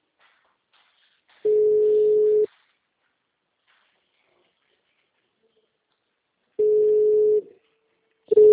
Ringback tone (Europe, 425 Hz).